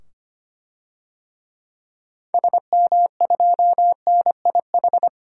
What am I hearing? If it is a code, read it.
Morse code, heard as a keyed tone, about 25 wpm, 700 Hz. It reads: SM2NIH
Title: DXCC-100 Call Signs - 6 Characters - 25wpm